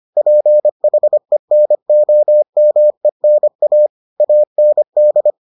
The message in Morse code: PHENOMENA AND

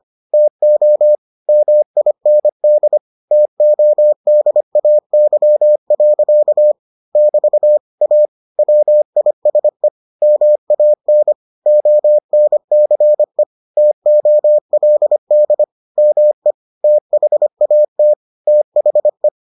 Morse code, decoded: TO MIND TODAY. <BT> A WISE MAN ONCE TOLD ME THAT THE

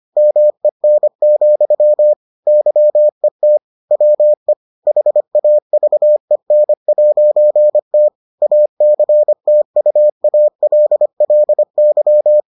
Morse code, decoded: MEN, YET WE HAVEN'T ACTUALLY

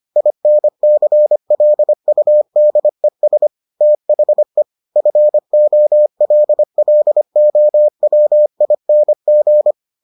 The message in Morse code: INCLUDES THE FOLLOWING

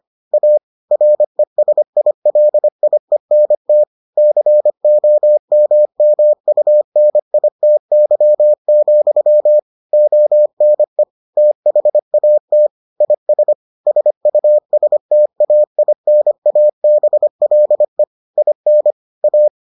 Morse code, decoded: A RESILIENT COMMUNITY, ONE THAT IS SUSTAINABLE IN A

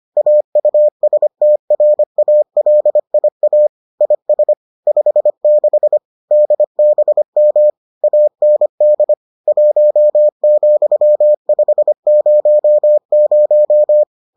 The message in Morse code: AUSTRALIA IS 56 DBM AND 1,500

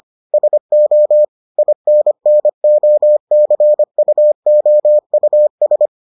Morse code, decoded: SO INNOCUOUS

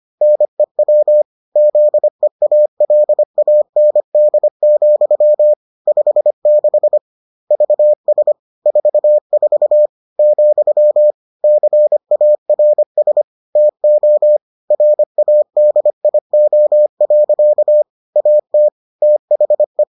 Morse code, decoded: NEW ZEALAND, 56 VS 44, CARS TO RADIO. AT THE

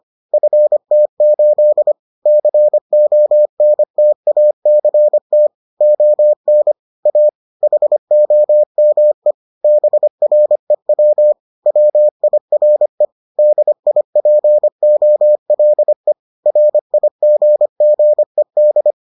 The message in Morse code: FT8 CONTACT ON A HOME BREW WIRE DIPOLE RIGGED

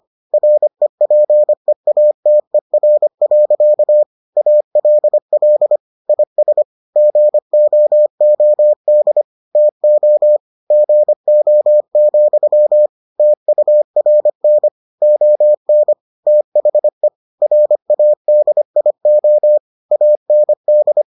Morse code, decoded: REPEATER. ALL IS GOOD TO GO, TURN ON THE RADIO AND

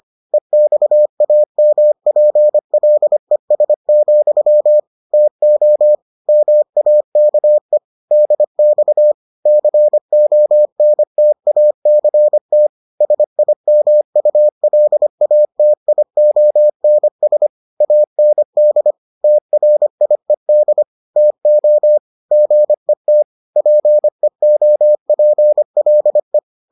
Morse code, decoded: EXAMPLES, TO MAKE DX CONTACT SIMULATIONS AND TRIED TO GET PEOPLE

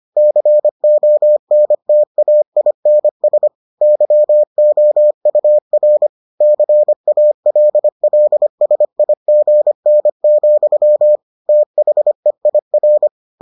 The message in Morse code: CONTAINS YOUR CALLSIGN, THEIR

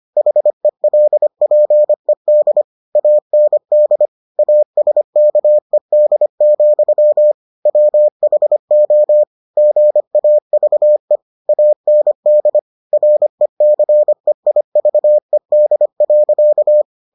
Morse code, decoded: HELPED AND ASKED, WHO GAVE AND RECEIVED.

